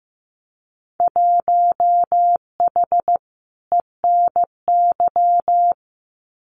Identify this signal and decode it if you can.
Morse code, heard as a keyed tone, about 15 wpm, 700 Hz. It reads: 1H ENY
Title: Winter Field Day Exchanges 15wpm